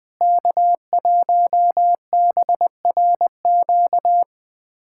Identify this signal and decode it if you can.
Morse code, heard as a keyed tone, about 20 wpm, 700 Hz. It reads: K1BRQ
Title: USA Callsign DE Callsign 20wpm